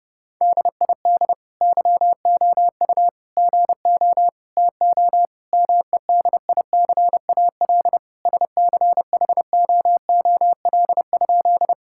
Morse code, heard as a keyed tone, about 30 wpm, 700 Hz. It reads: DID YOU GO TO MEDICAL SCHOOL?
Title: Sentences from Top 1000 Words - Review Part 5 - 30wpm